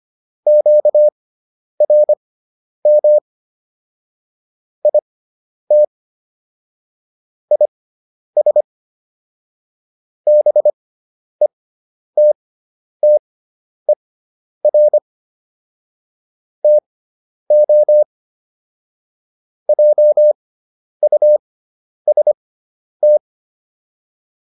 Morse code, heard as a keyed tone, about 25 wpm, spaced as for 10 wpm, 600 Hz. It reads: QRM IT IS BETTER TO JUST